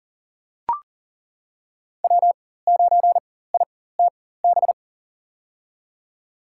Morse code, heard as a keyed tone, about 40 wpm, spaced as for 19 wpm, 700 Hz. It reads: W9ITB
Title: USA Call Signs - 4x Character Spacing - 40wpm